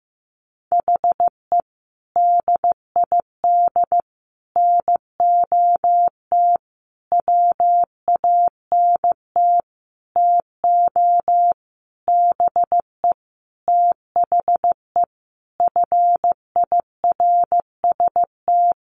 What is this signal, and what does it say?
Morse code, heard as a keyed tone, about 14 wpm, 700 Hz. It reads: HE DID NOT WANT TO BE THE FIRST
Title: Sentences from Top 1300 Words 15wpm